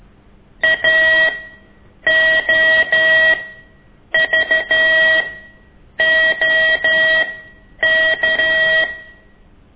Morse code, heard as a keyed tone, about 13 wpm, spaced as for 8 wpm, 1900 Hz. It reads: AOVOK
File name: ik1weg2 / CORSI CW